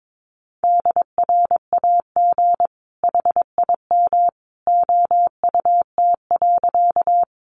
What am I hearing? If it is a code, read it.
Morse code, heard as a keyed tone, about 22 wpm, 700 Hz. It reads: DRAG HIM OUT.
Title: Sentences from Top 1500 Words 22wpm